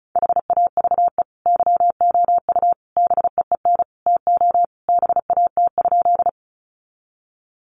Morse code, heard as a keyed tone, about 34 wpm, 700 Hz. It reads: HAVE YOU BEEN TO BAT?
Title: Sentences from Top 1500 Words 35wpm